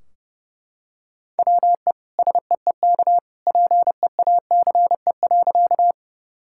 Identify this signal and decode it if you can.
Morse code, heard as a keyed tone, about 30 wpm, 700 Hz. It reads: WE SEEK PEACE.
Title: Sentences from Top 1600 Words 30wpm